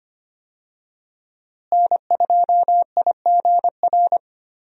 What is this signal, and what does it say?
Morse code, heard as a keyed tone, about 25 wpm, 700 Hz. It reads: N2IGR
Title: USA Call Signs - 5 Characters - Repeated 2x - 25wpm